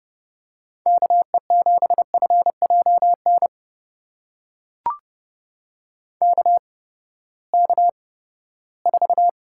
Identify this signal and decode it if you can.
Morse code, heard as a keyed tone, about 32 wpm, spaced as for 17 wpm, 700 Hz. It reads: KE7FJN KK4
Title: USA Call Signs - 6 Characters - 8x Character Spacing - 30wpm